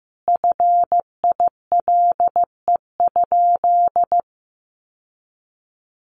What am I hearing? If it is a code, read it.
Morse code, heard as a keyed tone, about 14 wpm, 700 Hz. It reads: FILE?
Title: Sentences from Top 1200 Words 15wpm